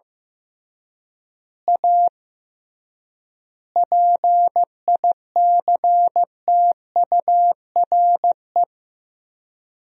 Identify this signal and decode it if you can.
Morse code, heard as a keyed tone, about 14 wpm, 700 Hz. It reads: A PICTURE
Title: 3x Word-Spacing - Sentences from Top 500 Words - 15wpm